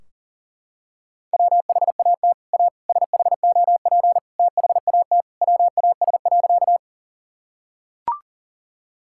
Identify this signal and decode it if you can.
Morse code, heard as a keyed tone, about 40 wpm, 700 Hz. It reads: WHAT A SHOP THAT WAS.
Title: Sentences from Top 1600 Words 40wpm